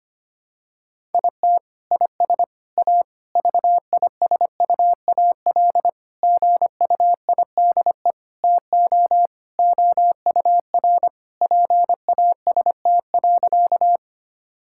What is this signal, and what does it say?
Morse code, heard as a keyed tone, about 25 wpm, 700 Hz. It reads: IT IS A VISUAL GUIDE TO OUR PAST.
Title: Sentences from Top 1000 Words - Review Part 6 - 25wpm